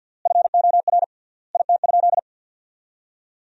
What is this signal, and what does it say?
Morse code, heard as a keyed tone, about 50 wpm, 700 Hz. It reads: FOR IT?
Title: 3x Word-Spacing - Sentences from Top 400 Words - 50wpm